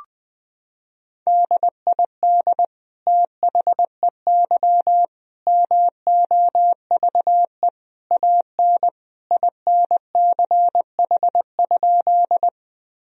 Morse code, heard as a keyed tone, about 20 wpm, 700 Hz. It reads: DID THEY MOVE AN INCH?
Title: Sentences from Top 1000 Words - Review Part 3 - 20wpm